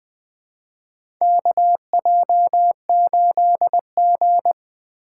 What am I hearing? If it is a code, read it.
Morse code, heard as a keyed tone, about 21 wpm, 700 Hz. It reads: KJ8G
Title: USA Call Signs - 4 Characters - 4x Character Spacing - 20wpm